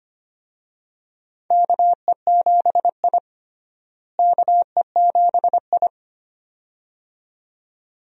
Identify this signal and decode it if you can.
Morse code, heard as a keyed tone, about 25 wpm, 700 Hz. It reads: KE7I KE7I
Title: USA Call Signs - 4 Characters - Repeated 2x - 25wpm